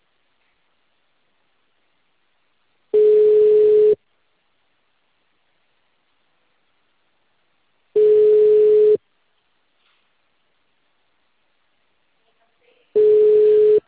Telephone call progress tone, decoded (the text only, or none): ringback tone